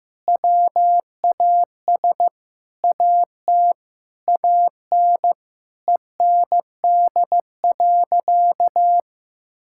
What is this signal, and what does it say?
Morse code, heard as a keyed tone, about 15 wpm, 700 Hz. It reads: WAS AT AN END.
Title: Sentences from Top 1000 Words - Review Part 6 - 15wpm